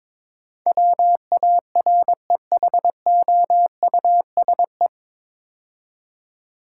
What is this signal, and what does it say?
Morse code, heard as a keyed tone, about 22 wpm, 700 Hz. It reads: WAREHOUSE
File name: 9 Letter Words - First 3 Letters Given - 22wpm